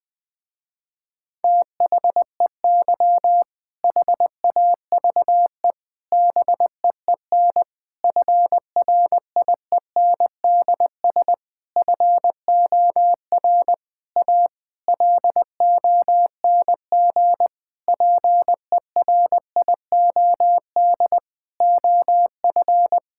Morse code, heard as a keyed tone, about 20 wpm, 700 Hz. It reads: THEY HAVE BEEN FRIENDS FOR A LONG PERIOD OF
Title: Sentences with 6-Word Phrases - Spoken Beginning - 20wpm